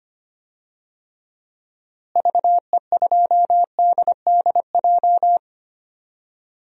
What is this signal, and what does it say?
Morse code, heard as a keyed tone, about 25 wpm, 700 Hz. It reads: VE2DDJ
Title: DXCC-100 Call Signs - 6 Characters - Repeated 3x - 25wpm